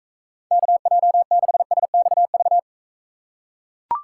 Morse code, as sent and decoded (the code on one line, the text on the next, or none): -.- .--- -.... ... -..- ...-
KJ6SXV